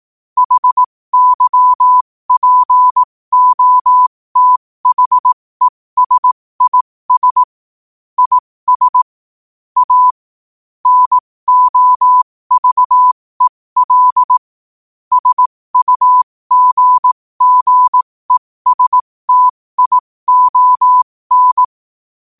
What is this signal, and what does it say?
Morse code, heard as a keyed tone, about 18 wpm, spaced as for 15 wpm, 1000 Hz. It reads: HYPOTHESIS IS A NOVEL SUGGESTION
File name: qotdfiles / 15wpm